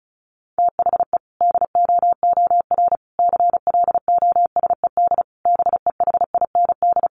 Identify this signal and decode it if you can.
Morse code, heard as a keyed tone, about 34 wpm, 700 Hz. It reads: THE DOOR CLOSED BEHIND
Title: Sentences from Top 1400 Words 35wpm